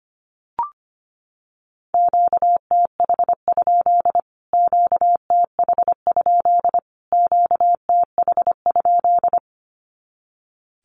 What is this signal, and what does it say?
Morse code, heard as a keyed tone, about 25 wpm, 700 Hz. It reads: QTH? QTH? QTH?
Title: Q-Codes Repeated 3x 25wpm